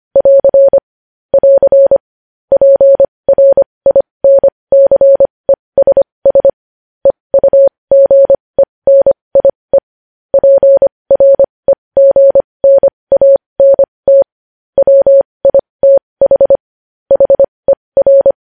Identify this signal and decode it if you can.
Morse code, heard as a keyed tone, about 25 wpm, spaced as for 20 wpm, 550 Hz. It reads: <AR> <AR> PRINCESS EUGENIE PREGNANT WITH HER
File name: podcast / News headlines-1x-25x20